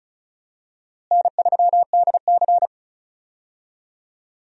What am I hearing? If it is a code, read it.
Morse code, heard as a keyed tone, about 36 wpm, 700 Hz. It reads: N3DC
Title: USA Call Signs - 4 Characters - Repeated 2x - 35wpm